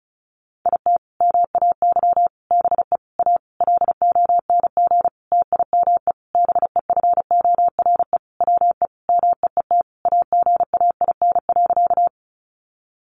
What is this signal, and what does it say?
Morse code, heard as a keyed tone, about 34 wpm, 700 Hz. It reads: IT MAY BE A LONG TIME BEFORE WE MEET AGAIN.
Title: Sentences from Top 1100 Words 35wpm